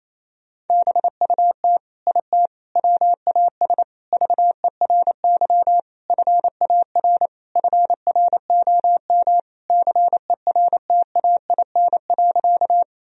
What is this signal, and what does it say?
Morse code, heard as a keyed tone, about 28 wpm, 700 Hz. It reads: BUT IT WAS VERY FAR FROM CERTAIN.
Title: Sentences from Top 1000 Words - Review Part 2 - 28wpm